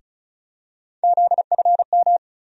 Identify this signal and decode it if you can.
Morse code, heard as a keyed tone, about 34 wpm, 700 Hz. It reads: ZFM
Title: USA Call Sign Postfixes - 3 Letters 35wpm